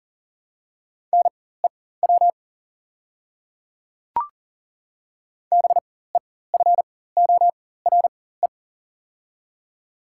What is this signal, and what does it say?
Morse code, heard as a keyed tone, about 40 wpm, spaced as for 19 wpm, 700 Hz. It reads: NEW BEFORE
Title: 4x Character Spacing - Top 200 Words - 40wpm